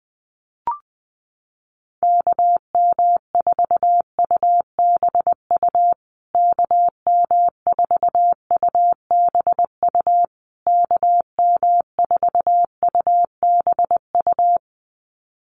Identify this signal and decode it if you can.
Morse code, heard as a keyed tone, about 20 wpm, 700 Hz. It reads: KM4UBU KM4UBU KM4UBU
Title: USA Call Signs Repeated 3x 20wpm